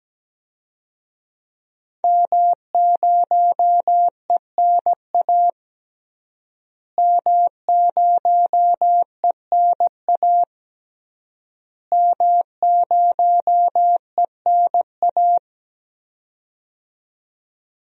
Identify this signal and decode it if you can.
Morse code, heard as a keyed tone, about 17 wpm, 700 Hz. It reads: M0ENA M0ENA M0ENA
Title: DXCC-100 Call Signs - 5 Characters - Repeated 3x - 17wpm